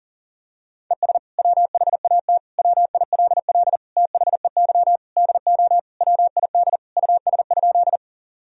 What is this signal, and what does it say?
Morse code, heard as a keyed tone, about 40 wpm, 700 Hz. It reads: ES WHAT WILL THEY DO WID US?
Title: Sentences from Top 200 Words and Abbreviations 40wpm